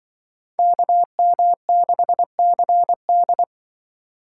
Morse code, heard as a keyed tone, about 25 wpm, 700 Hz. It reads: KM6CD
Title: Speed Racing - USA 2x2 Call Signs 20wpm